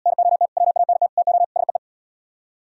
Morse code, heard as a keyed tone, about 72 wpm, 700 Hz. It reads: NOT WANT TO SEE